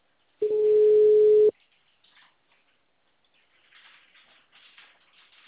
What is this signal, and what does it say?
Ringback tone (Europe, 425 Hz).